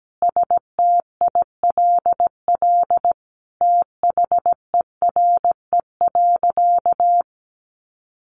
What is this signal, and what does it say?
Morse code, heard as a keyed tone, about 17 wpm, 700 Hz. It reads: STILL THERE.
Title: Sentences from Top 1500 Words 17wpm